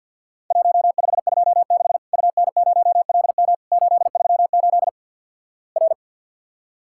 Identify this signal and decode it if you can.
Morse code, heard as a keyed tone, about 48 wpm, 700 Hz. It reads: 1526 UN0LM 838 R
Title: Worked All Europe DX Contest - QTC Exchange - 50wpm